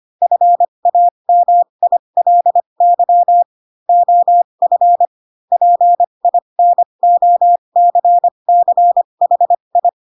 Morse code, heard as a keyed tone, about 25 wpm, spaced as for 22 wpm, 700 Hz. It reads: FAMILY OF PINOCCHI
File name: Pinocchio / Pinocchio ch3 18 25 ik1hge